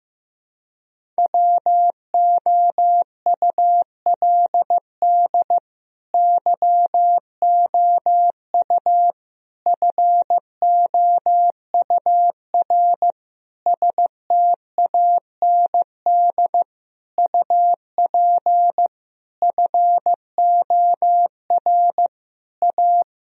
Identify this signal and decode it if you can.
Morse code, heard as a keyed tone, about 15 wpm, 700 Hz. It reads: WOULD YOU FOUR STAND UP FOR A
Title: Sentences from Top 1000 Words - Review Part 1 - 15wpm